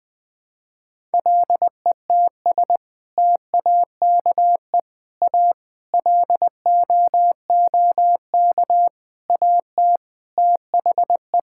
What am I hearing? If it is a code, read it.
Morse code, heard as a keyed tone, about 21 wpm, 700 Hz. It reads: LETS TAKE A LOOK AT THE
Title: Sentences with 6-Word Phrases - Spoken Beginning - 20wpm